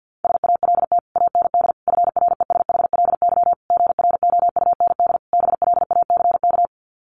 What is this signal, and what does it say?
Morse code, heard as a keyed tone, about 50 wpm, 700 Hz. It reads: SALT AND FRESHLY GROUND BLACK
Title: Sets of 5 Words Encore 50wpm